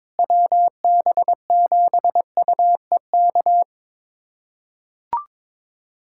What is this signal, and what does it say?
Morse code, heard as a keyed tone, about 22 wpm, 700 Hz. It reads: WB7UEK
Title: USA Callsign DE Callsign 22wpm